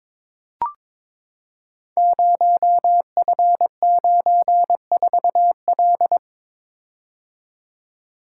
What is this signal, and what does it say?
Morse code, heard as a keyed tone, about 22 wpm, 700 Hz. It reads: 0F94L
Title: Code Groups - 5 Letter:Numbers - 22wpm